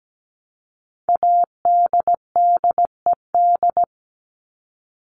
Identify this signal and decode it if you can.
Morse code, heard as a keyed tone, about 17 wpm, 700 Hz. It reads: ADDED
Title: ED Suffix - Top 100 Words with Base Words 17wpm